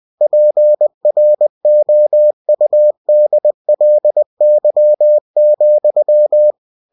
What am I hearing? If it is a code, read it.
Morse code, heard as a keyed tone, about 20 wpm, 600 Hz. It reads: PROUDLY,
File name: cw-books / winnie-the-pooh / winnie-20wpm-page-0064